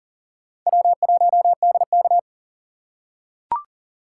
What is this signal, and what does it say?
Morse code, heard as a keyed tone, about 40 wpm, 700 Hz. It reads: W1DK
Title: USA Call Signs - 4 Characters - 2x Character Spacing - 40wpm